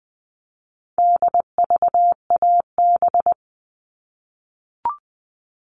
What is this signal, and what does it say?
Morse code, heard as a keyed tone, about 20 wpm, 700 Hz. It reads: DVAB
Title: All Cut Numbers - 4-Digit 20wpm